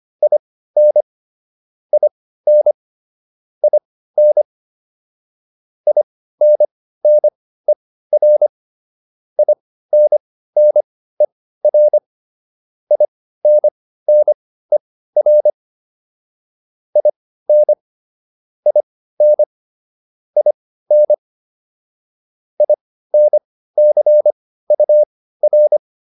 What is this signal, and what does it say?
Morse code, heard as a keyed tone, about 25 wpm, spaced as for 15 wpm, 600 Hz. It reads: IN IN IN INNER INNER INNER IN IN IN INCUR
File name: IN5 15